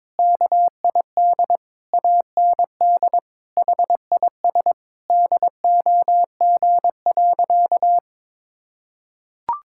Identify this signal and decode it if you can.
Morse code, heard as a keyed tone, about 22 wpm, 700 Hz. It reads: KID AND HIS DOG.
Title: Sentences from Top 1600 Words 22wpm